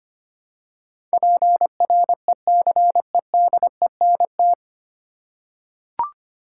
Morse code, heard as a keyed tone, about 25 wpm, 700 Hz. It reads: PRECEDENT
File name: practice / 25wpm / 9 Letter Words - First 3 Letters Given - 25wpm